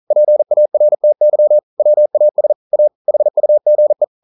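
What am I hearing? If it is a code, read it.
Morse code, heard as a keyed tone, about 41 wpm, 600 Hz. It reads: PARTY WAS A HUGE
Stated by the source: Morse code practice at 41 wpm, from CW med Gnister.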